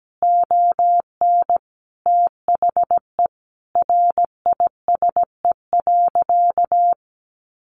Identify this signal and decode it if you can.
Morse code, heard as a keyed tone, about 17 wpm, 700 Hz. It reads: ON THE RISE.
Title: Sentences from Top 1300 Words 17wpm